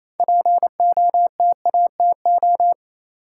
Morse code, heard as a keyed tone, about 28 wpm, 700 Hz. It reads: POTATO
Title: Six Letter Words - 28wpm